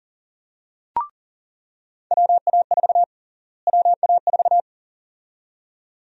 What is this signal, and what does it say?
Morse code, heard as a keyed tone, about 40 wpm, 700 Hz. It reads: WA4 WA4
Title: USA Call Sign Prefixes - Repeated 2x - 40wpm